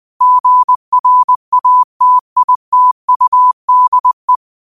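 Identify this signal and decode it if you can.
Morse code, heard as a keyed tone, about 20 wpm, 1000 Hz. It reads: GRATITUDE
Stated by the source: Morse code practice at 20 wpm: famous quotations sent in telegraphy, from URVAG.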